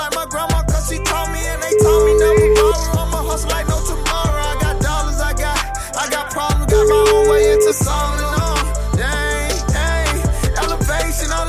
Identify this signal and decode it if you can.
Ringback tone (Europe, 425 Hz), with noise.